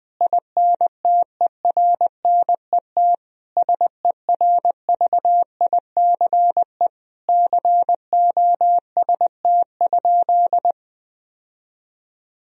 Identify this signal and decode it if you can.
Morse code, heard as a keyed tone, about 20 wpm, 700 Hz. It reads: INTERNET SERVICE COST?
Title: Sentences from Top 1000 Words - Review Part 4 - 20wpm